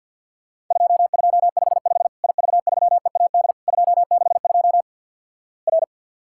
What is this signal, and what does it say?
Morse code, heard as a keyed tone, about 50 wpm, 700 Hz. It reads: 2255 IV3EAD 262 R
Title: Worked All Europe DX Contest - QTC Exchange - 50wpm